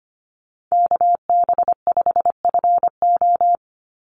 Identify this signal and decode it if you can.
Morse code, heard as a keyed tone, about 25 wpm, 700 Hz. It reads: KB5FO
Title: USA Call Signs 25wpm